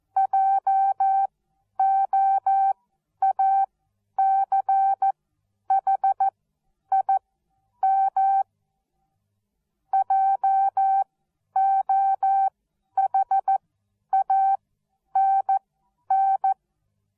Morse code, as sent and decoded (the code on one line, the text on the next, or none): .--- --- .- -.-. .... .. -- / .--- --- .... .- -. -.
JOACHIM JOHANN